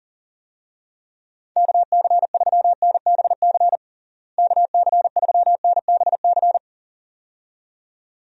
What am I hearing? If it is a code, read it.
Morse code, heard as a keyed tone, about 40 wpm, 700 Hz. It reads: KC3NBC KC3NBC
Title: USA Call Signs - 6 Characters - Repeated 2x - 40wpm